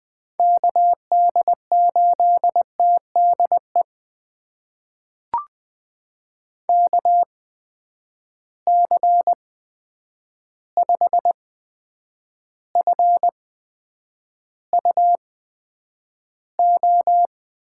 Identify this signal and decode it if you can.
Morse code, heard as a keyed tone, about 20 wpm, spaced as for 9 wpm, 700 Hz. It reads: KD8TDE KC5FUO
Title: USA Call Signs - 6 Characters - 8x Character Spacing - 20wpm